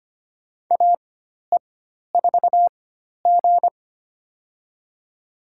Morse code, heard as a keyed tone, about 25 wpm, spaced as for 12 wpm, 700 Hz. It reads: AE4G
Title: USA Call Signs - 4 Characters - 4x Character Spacing - 25wpm